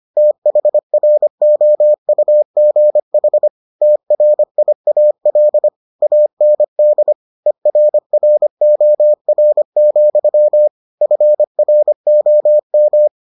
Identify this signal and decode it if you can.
Morse code, heard as a keyed tone, about 25 wpm, 600 Hz. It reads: THROUGH TRIAL AND ERROR, FROM